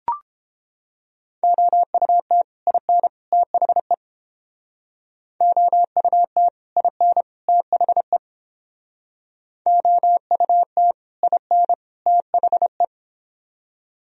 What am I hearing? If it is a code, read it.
Morse code, heard as a keyed tone, about 28 wpm, 700 Hz. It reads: OUT IN THE OUT IN THE OUT IN THE